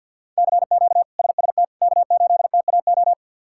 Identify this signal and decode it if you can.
Morse code, heard as a keyed tone, about 46 wpm, 700 Hz. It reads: CQ SST K8TAO
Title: K1USN SST QSO Practice - 50wpm